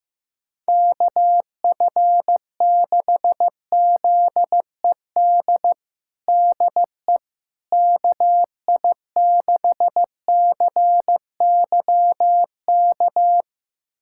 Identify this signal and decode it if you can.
Morse code, heard as a keyed tone, about 15 wpm, 700 Hz. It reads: KF6ZED DE KI6CYK
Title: USA Callsign DE Callsign - 6 Characters - 15wpm